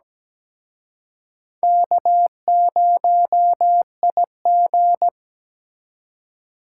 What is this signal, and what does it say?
Morse code, heard as a keyed tone, about 20 wpm, 700 Hz. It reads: K0IG
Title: Speed Racing - USA 1x2 Call Signs 17wpm